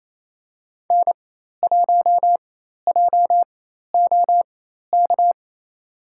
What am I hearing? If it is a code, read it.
Morse code, heard as a keyed tone, about 28 wpm, spaced as for 13 wpm, 700 Hz. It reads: N1JOK
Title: USA Call Signs - 5 Characters - 4x Character Spacing - 28wpm